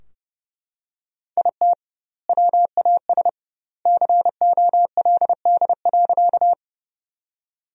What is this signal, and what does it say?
Morse code, heard as a keyed tone, about 30 wpm, 700 Hz. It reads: IT WAS COLD.
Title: 2x Word-Spacing - Sentences from Top 400 Words - 30wpm